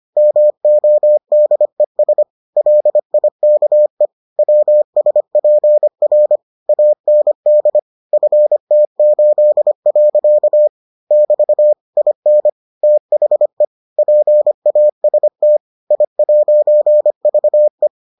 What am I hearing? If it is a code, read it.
Morse code, heard as a keyed tone, about 25 wpm, 600 Hz. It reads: MODES LIKE WSPR AND FT8. <BT> IN THE PAST I'VE